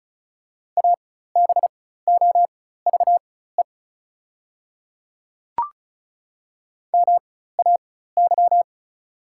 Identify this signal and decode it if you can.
Morse code, heard as a keyed tone, about 36 wpm, spaced as for 16 wpm, 700 Hz. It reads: ABOVE MAY
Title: 4x Character Spacing - Top 500 Words - 35wpm